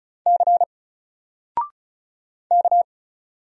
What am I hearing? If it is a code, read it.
Morse code, heard as a keyed tone, about 36 wpm, 700 Hz. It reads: C K
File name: Single-Letter-Number-35wpm